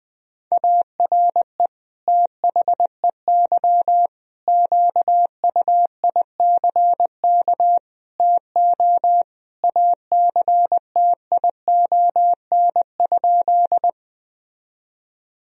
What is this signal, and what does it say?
Morse code, heard as a keyed tone, about 21 wpm, 700 Hz. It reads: ARE THEY QUICK TO ACTION?
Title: Sentences from Top 1000 Words - Review Part 4 - 20wpm